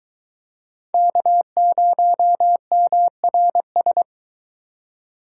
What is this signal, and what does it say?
Morse code, heard as a keyed tone, about 21 wpm, 700 Hz. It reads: K0MRS